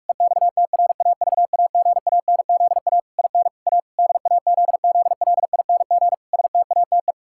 Morse code, heard as a keyed tone, about 48 wpm, 700 Hz. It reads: EXTRAVAGANZA IN A DAZZLING STATE
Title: Pangram Challenge v2 - 45wpm